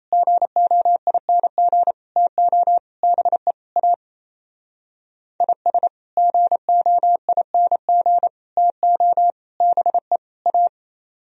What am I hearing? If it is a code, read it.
Morse code, heard as a keyed tone, about 36 wpm, 700 Hz. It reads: GOING TO BE A IS GOING TO BE A